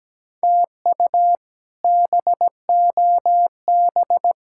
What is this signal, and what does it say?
Morse code, heard as a keyed tone, about 17 wpm, 700 Hz. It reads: TU BOB
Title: K1USN SST QSO Practice - 17wpm